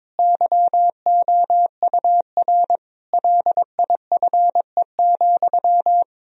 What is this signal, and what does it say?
Morse code, heard as a keyed tone, about 22 wpm, 700 Hz. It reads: YOUR LIFE,
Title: Sentences from Top 1000 Words - Review Part 3 - 22wpm